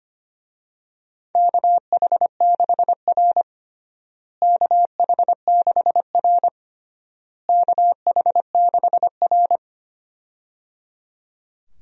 Morse code, heard as a keyed tone, about 25 wpm, 700 Hz. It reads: KH6R KH6R KH6R
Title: USA Call Signs - 4 Characters - Repeated 3x - 25wpm